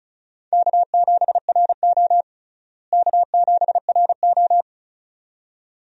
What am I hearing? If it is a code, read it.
Morse code, heard as a keyed tone, about 34 wpm, 700 Hz. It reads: K7RO K7RO
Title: USA Call Signs - 4 Characters - Repeated 2x - 35wpm